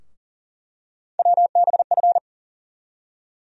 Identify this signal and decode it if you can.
Morse code, heard as a keyed tone, about 37 wpm, 700 Hz. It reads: WBF
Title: USA Call Sign Postfixes - 3 Letters 40wpm